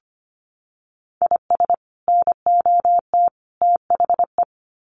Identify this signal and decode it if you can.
Morse code, heard as a keyed tone, about 25 wpm, 700 Hz. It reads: IS NOT THE
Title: Sets of 3 Words 25wpm